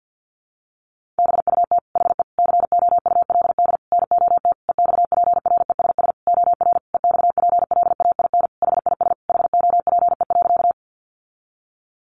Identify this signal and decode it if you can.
Morse code, heard as a keyed tone, about 50 wpm, 700 Hz. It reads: BUT HE COULD NOT EXPRESS OR EXPLAIN HIS HOPE.
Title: Sentences from Top 1100 Words 50wpm